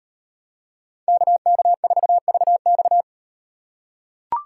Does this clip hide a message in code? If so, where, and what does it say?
Yes: Morse code from 1.1 to 3.0 s, heard as a keyed tone, about 38 wpm, 700 Hz. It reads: KK4VX